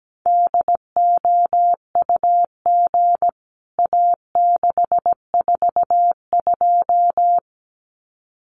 Morse code, heard as a keyed tone, about 17 wpm, 700 Hz. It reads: DOUG A642
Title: CWT Exchange 17wpm